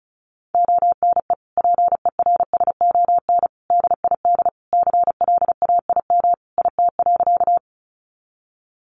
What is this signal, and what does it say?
Morse code, heard as a keyed tone, about 34 wpm, 700 Hz. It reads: ONE PERSON DID CLAIM IT.
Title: Sentences from Top 1500 Words 35wpm